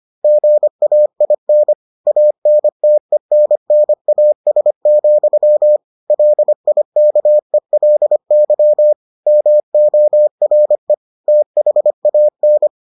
Morse code, heard as a keyed tone, about 25 wpm, 600 Hz. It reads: GAIN ANTENNAS, LIKELY MORE THAN